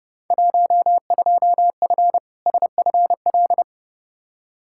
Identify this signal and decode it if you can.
Morse code, heard as a keyed tone, about 30 wpm, 700 Hz. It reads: 12F SFL
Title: Field Day Exchanges Encore - 30wpm